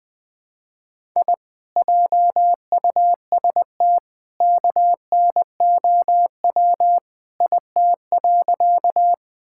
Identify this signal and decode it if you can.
Morse code, heard as a keyed tone, about 21 wpm, 700 Hz. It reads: I JUST KNOW IT.
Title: Sentences with 2-Word Phrases - Spoken Beginning - 20wpm